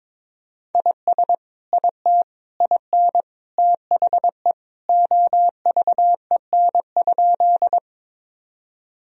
Morse code, heard as a keyed tone, about 22 wpm, 700 Hz. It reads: IS IT IN THE OVEN?
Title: Sentences from Top 1000 Words - Review Part 4 - 22wpm